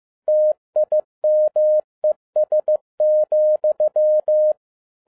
Morse code, heard as a keyed tone, about 15 wpm, 610 Hz. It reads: TIMES,
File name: 2011-07-mp3 / n47 JOHN K3TN 15wpm